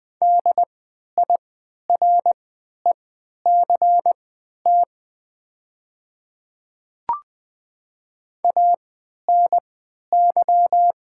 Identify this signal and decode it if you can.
Morse code, heard as a keyed tone, about 20 wpm, spaced as for 11 wpm, 700 Hz. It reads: DIRECT ANY
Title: 3x Character Spacing - Top 500 Words - 20wpm